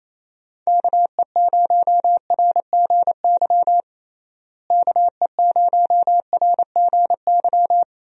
Morse code, heard as a keyed tone, about 28 wpm, 700 Hz. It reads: KE0RGY KE0RGY
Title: USA Call Signs - 6 Characters - Repeated 2x - 28wpm